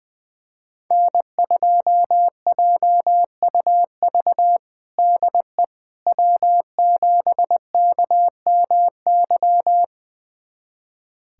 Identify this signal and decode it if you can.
Morse code, heard as a keyed tone, about 20 wpm, 700 Hz. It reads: N2JUV DE W7KMY
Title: USA Callsign DE Callsign - 5 Characters - 20wpm